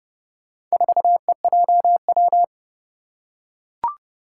Morse code, heard as a keyed tone, about 28 wpm, 700 Hz. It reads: 4EJW
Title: Four Challenging Letter-Number Combinations 30wpm